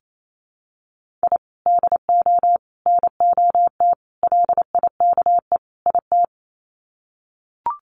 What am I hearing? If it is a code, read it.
Morse code, heard as a keyed tone, about 28 wpm, 700 Hz. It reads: I DO NOT LIKE IT
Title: Sets of 5 Words Encore 28wpm